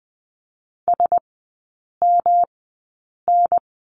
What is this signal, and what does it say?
Morse code, heard as a keyed tone, about 20 wpm, spaced as for 8 wpm, 700 Hz. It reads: SMN
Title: 184 Lesson 30e - Letter b - Three Characters 20wpm